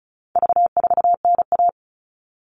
Morse code, heard as a keyed tone, about 34 wpm, 700 Hz. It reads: V4NA
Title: All Cut Numbers - 4-Digit 35wpm